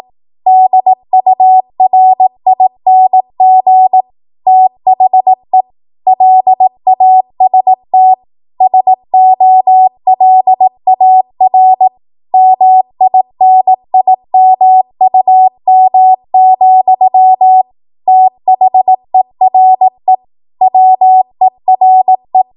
Morse code, heard as a keyed tone, about 18 wpm, 750 Hz. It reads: DURING THE LAST SOLAR MINIMUM, THERE WERE
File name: Archive / 18 WPM / 18WPM